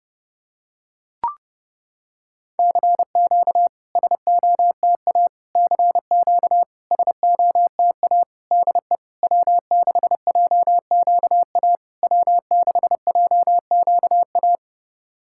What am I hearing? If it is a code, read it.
Morse code, heard as a keyed tone, about 30 wpm, 690 Hz. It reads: CQ SOTA CQ SOTA DE W6JQA W6JQA
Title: SOTA QSO Practice 30wpm